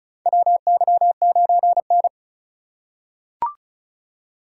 Morse code, heard as a keyed tone, about 36 wpm, 700 Hz. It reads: WY9N
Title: USA Call Signs - 3x Character Spacing - 35wpm